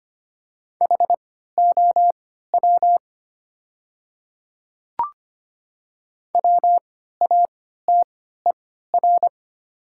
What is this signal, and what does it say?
Morse code, heard as a keyed tone, about 25 wpm, spaced as for 14 wpm, 700 Hz. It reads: HOW WATER
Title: 3x Character Spacing - Top 100 Words - 25wpm